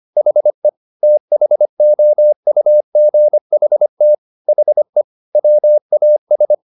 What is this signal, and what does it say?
Morse code, heard as a keyed tone, about 25 wpm, 600 Hz. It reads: HE THOUGHT HE WAS